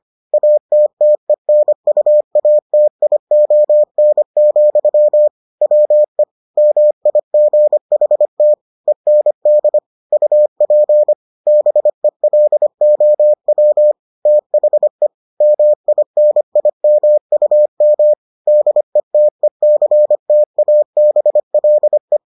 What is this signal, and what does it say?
Morse code, heard as a keyed tone, about 25 wpm, 600 Hz. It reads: ATTENUATION, WE MIGHT END UP BELOW THE MINIMUM DETECTABLE